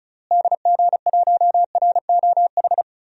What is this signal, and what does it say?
Morse code, heard as a keyed tone, about 34 wpm, 700 Hz. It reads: DG1ROH
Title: DXCC-100 Call Signs - 6 Characters - Repeated 2x - 35wpm